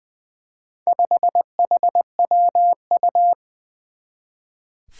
Morse code, heard as a keyed tone, about 20 wpm, 700 Hz. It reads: 5HWU
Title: Four Challenging Letter-Number Combinations 20wpm